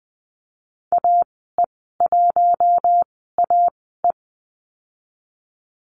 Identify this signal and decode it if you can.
Morse code, heard as a keyed tone, about 20 wpm, spaced as for 15 wpm, 700 Hz. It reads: AE1AE
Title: 076 Lesson 13g - Letter L - Call Signs 20wpm